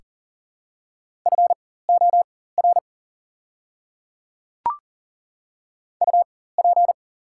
Morse code, heard as a keyed tone, about 42 wpm, spaced as for 19 wpm, 700 Hz. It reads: FOR UP
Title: 4x Character Spacing - Top 100 Words - 40wpm